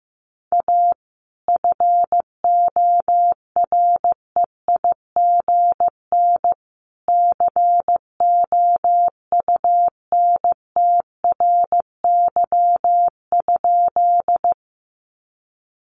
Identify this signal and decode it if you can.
Morse code, heard as a keyed tone, about 15 wpm, 700 Hz. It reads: A FOREIGN COUNTRY?
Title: Sentences from Top 1400 Words 15wpm